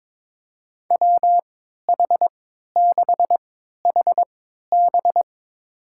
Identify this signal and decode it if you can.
Morse code, heard as a keyed tone, about 22 wpm, spaced as for 13 wpm, 700 Hz. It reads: WH6HB
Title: USA Call Signs - 5 Characters - 3x Character Spacing - 22wpm